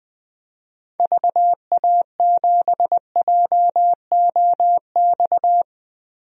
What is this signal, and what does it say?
Morse code, heard as a keyed tone, about 20 wpm, 700 Hz. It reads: VA7JOX
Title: DXCC-100 Call Signs - 6 Characters - 20wpm